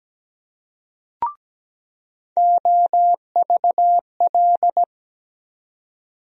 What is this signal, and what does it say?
Morse code, heard as a keyed tone, about 17 wpm, 700 Hz. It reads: OVL
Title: USA Call Sign Postfixes - 3 Letters 17wpm